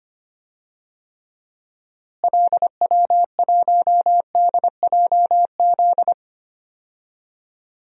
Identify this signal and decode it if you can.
Morse code, heard as a keyed tone, about 25 wpm, 700 Hz. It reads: LW1DJZ
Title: DXCC-100 Call Signs - 6 Characters - Repeated 2x - 25wpm